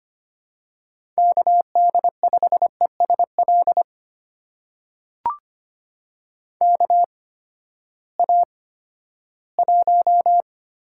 Morse code, heard as a keyed tone, about 25 wpm, spaced as for 14 wpm, 700 Hz. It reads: KD5ESL KA1
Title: USA Call Signs - 6 Characters - 8x Character Spacing - 25wpm